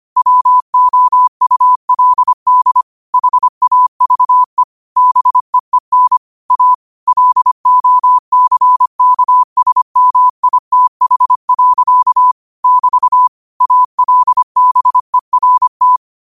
Morse code, heard as a keyed tone, about 25 wpm, 1000 Hz. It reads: WOULD HAVE BEEN A LOCKSMITH. <BT> ALBERT
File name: qotdfiles / 25wpm